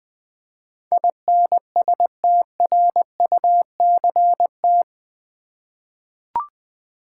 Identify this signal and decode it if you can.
Morse code, heard as a keyed tone, about 20 wpm, 700 Hz. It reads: INSTRUCT
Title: 8 Letter Words - First 3 Letters Given - 20wpm